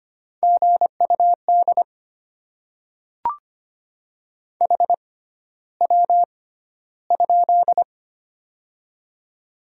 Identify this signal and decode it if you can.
Morse code, heard as a keyed tone, about 25 wpm, spaced as for 13 wpm, 700 Hz. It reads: GUD HW?
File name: Top 20 QSO Elements - 6x Character Spacing - 25wpm